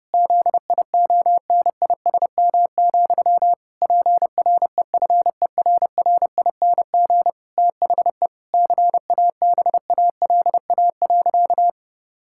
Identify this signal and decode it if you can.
Morse code, heard as a keyed tone, about 30 wpm, 700 Hz. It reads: ZIONISM, PREFERRING THE CABALA.
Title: Pangram Challenge v2 - 30wpm